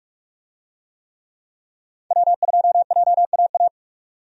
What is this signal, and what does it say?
Morse code, heard as a keyed tone, about 42 wpm, 700 Hz. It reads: W2JAA
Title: USA Call Signs - 5 Characters - 45wpm